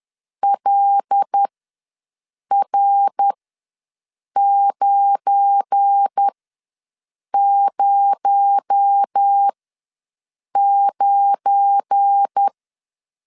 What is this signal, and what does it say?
Morse code, heard as a keyed tone, about 10 wpm, spaced as for 6 wpm, 800 Hz. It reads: LR909
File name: LEZIONI / 9 286 ws 10wpm 4032